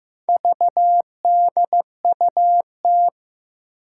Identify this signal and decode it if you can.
Morse code, heard as a keyed tone, about 15 wpm, 700 Hz. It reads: VDUT
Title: Four Challenging Letter-Number Combinations 15wpm